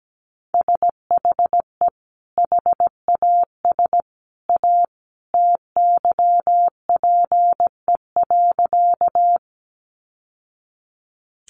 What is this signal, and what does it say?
Morse code, heard as a keyed tone, about 17 wpm, 700 Hz. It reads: SHE HAS A TYPE.
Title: Sentences from Top 600 Words 17wpm